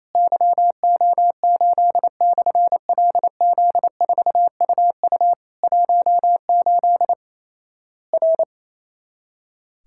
Morse code, heard as a keyed tone, about 28 wpm, 700 Hz. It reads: YO8/LZ4UU 18 R
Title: Worked All Europe DX Contest - QTC Exchange - 28wpm